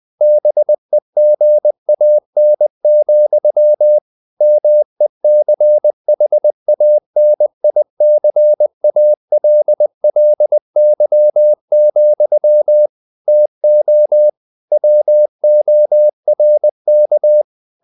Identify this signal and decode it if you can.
Morse code, heard as a keyed tone, about 20 wpm, 600 Hz. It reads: BEGAN, MECHANICALLY, TO WORK
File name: the old man and the sea / the old man and the sea part 1 20wpm